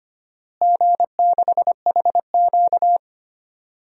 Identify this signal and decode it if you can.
Morse code, heard as a keyed tone, about 25 wpm, 700 Hz. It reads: G6HQ
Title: DXCC-100 Call Signs - 4 Characters - Repeated 3x - 25wpm